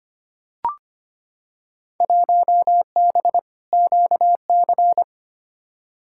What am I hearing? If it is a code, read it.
Morse code, heard as a keyed tone, about 25 wpm, 700 Hz. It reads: 1B QC
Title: Field Day Exchanges Encore - 25wpm